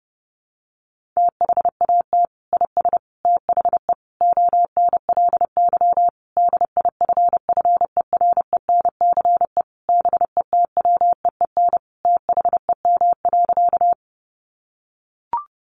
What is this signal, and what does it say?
Morse code, heard as a keyed tone, about 30 wpm, 700 Hz. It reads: THAT IS THE ONLY DIFFERENCE BETWEEN THEM.
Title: Sentences from Top 800 Words 30wpm